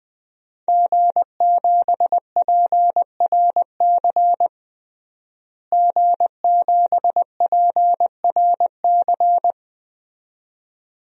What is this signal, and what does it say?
Morse code, heard as a keyed tone, about 21 wpm, 700 Hz. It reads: G7PRC G7PRC
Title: DXCC-100 Call Signs - 5 Characters - Repeated 2x - 20wpm